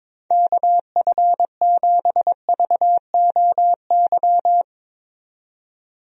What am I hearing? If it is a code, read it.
Morse code, heard as a keyed tone, about 22 wpm, 700 Hz. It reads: KF7VOY
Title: USA Call Signs - 6 Characters - 22wpm